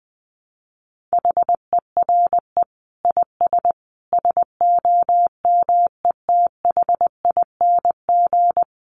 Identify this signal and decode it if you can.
Morse code, heard as a keyed tone, about 20 wpm, 700 Hz. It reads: HERE IS SOMETHING
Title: Sentences from Top 1000 Words 20wpm